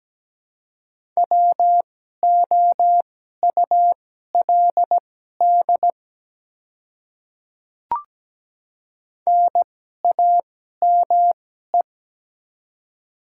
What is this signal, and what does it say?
Morse code, heard as a keyed tone, about 17 wpm, spaced as for 12 wpm, 700 Hz. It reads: WOULD NAME
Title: 2x Character Spacing - Top 200 Words - 17wpm